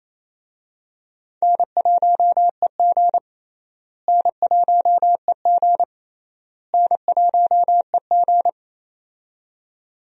Morse code, heard as a keyed tone, about 28 wpm, 700 Hz. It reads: N1EG N1EG N1EG
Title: USA Call Signs - 4 Characters - Repeated 3x - 28wpm